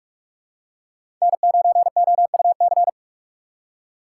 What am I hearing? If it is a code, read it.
Morse code, heard as a keyed tone, about 42 wpm, 700 Hz. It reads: N9OUC
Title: USA Call Signs - 5 Characters - 4x Character Spacing - 45wpm